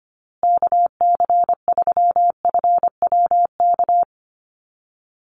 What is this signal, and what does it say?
Morse code, heard as a keyed tone, about 25 wpm, 700 Hz. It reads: KC3FWK
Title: USA Call Signs Repeated 2x 25wpm